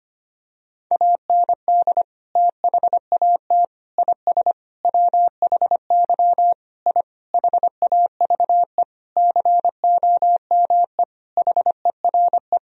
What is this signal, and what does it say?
Morse code, heard as a keyed tone, about 25 wpm, 700 Hz. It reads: AND THAT IS WHY I HAVE COME HERE
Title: Sentences from Top 1000 Words - Review Part 3 - 25wpm